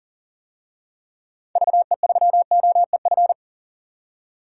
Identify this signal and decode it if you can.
Morse code, heard as a keyed tone, about 40 wpm, 700 Hz. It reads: VE3OEF
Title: DXCC-100 Call Signs - 6 Characters - 40wpm